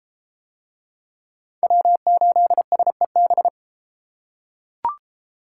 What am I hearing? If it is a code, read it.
Morse code, heard as a keyed tone, about 32 wpm, 700 Hz. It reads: W8SEB